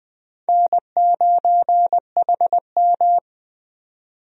Morse code, heard as a keyed tone, about 21 wpm, 700 Hz. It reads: N9HM
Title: USA Call Signs - 4 Characters - 20wpm